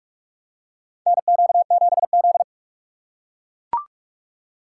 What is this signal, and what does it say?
Morse code, heard as a keyed tone, about 43 wpm, 700 Hz. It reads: NQ7Z
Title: USA Call Signs - 4 Characters - 6x Character Spacing - 45wpm